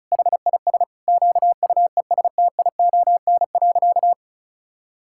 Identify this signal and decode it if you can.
Morse code, heard as a keyed tone, about 36 wpm, 700 Hz. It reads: HIS QUESTION.
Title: Sentences from Top 1000 Words - Review Part 6 - 35wpm